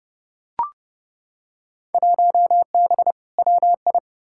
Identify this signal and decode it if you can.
Morse code, heard as a keyed tone, about 30 wpm, 700 Hz. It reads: 1B WI
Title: Field Day Exchanges - 30wpm